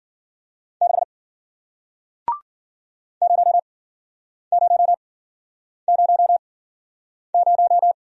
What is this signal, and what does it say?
Morse code, heard as a keyed tone, about 53 wpm, spaced as for 9 wpm, 700 Hz. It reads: 6 0000